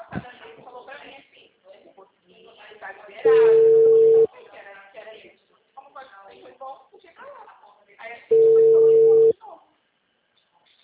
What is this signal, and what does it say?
Ringback tone (China, 450 Hz), with speech.
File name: out- -2028- - - .2384